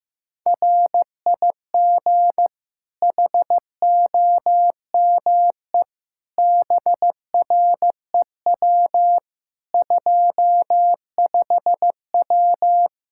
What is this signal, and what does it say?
Morse code, heard as a keyed tone, about 15 wpm, 700 Hz. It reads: RIG HOME BREW 25W
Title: K1USN SST QSO Practice 15wpm